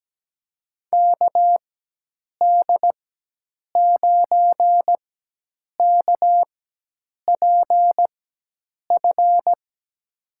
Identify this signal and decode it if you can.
Morse code, heard as a keyed tone, about 17 wpm, spaced as for 8 wpm, 700 Hz. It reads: KD9KPF
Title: USA Call Signs - 6 Characters - 4x Character Spacing - 17wpm